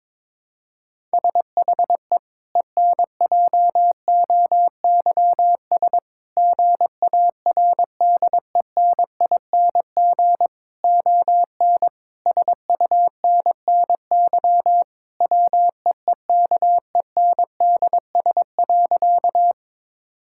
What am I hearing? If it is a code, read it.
Morse code, heard as a keyed tone, about 22 wpm, 700 Hz. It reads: SHE ENJOYS GARDENING ON SUNNY WEEKENDS.
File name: Difficult Sentences with Hints - 22wpm